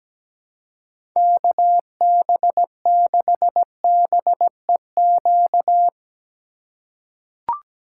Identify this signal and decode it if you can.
Morse code, heard as a keyed tone, about 17 wpm, 700 Hz. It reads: KB6BEQ
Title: USA Call Signs - 6 Characters - 3x Character Spacing - 17wpm